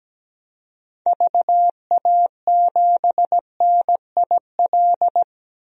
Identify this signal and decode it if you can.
Morse code, heard as a keyed tone, about 17 wpm, 700 Hz. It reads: VA7NIL
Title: DXCC-100 Call Signs - 6 Characters - 17wpm